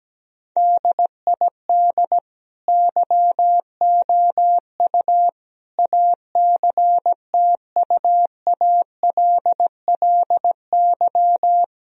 Morse code, heard as a keyed tone, about 17 wpm, 700 Hz. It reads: DID YOU ACTUALLY
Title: Sentences from Top 1000 Words - Review Part 4 - 17wpm